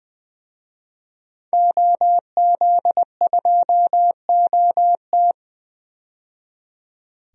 Morse code, heard as a keyed tone, about 20 wpm, 700 Hz. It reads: OZ2OT
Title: DXCC-100 Call Signs - 5 Characters - 20wpm